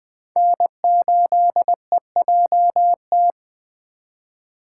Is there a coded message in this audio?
Yes: Morse code from 0.4 to 3.3 s, heard as a keyed tone, about 21 wpm, 700 Hz. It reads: N8EJT